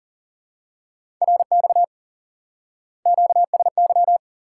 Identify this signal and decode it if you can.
Morse code, heard as a keyed tone, about 38 wpm, 700 Hz. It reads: RX QSY
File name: IQE Course 3 - Lesson 89 - YAGI - Two Elements - 40wpm